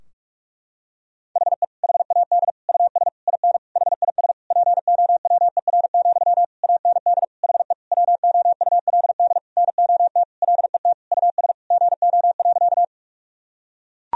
Morse code, heard as a keyed tone, about 42 wpm, 700 Hz. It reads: HE HAD US IN HIS POWER, AND HE WOULD NOT LET US GO.
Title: Sentences from Top 1000 Words - Review Part 2 - 45wpm